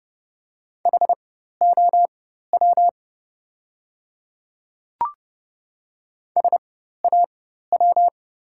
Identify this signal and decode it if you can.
Morse code, heard as a keyed tone, about 30 wpm, spaced as for 14 wpm, 700 Hz. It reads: HOW SAW
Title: 4x Character Spacing - Top 500 Words - 30wpm